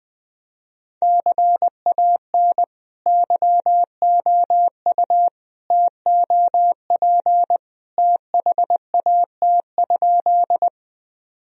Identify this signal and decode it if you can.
Morse code, heard as a keyed tone, about 21 wpm, 700 Hz. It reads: CAN YOU TOP THAT?
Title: Sentences from Top 1000 Words - Review Part 2 - 20wpm